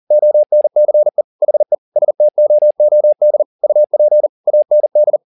Morse code, heard as a keyed tone, about 40 wpm, 600 Hz. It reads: ONCE HE STOOD UP AND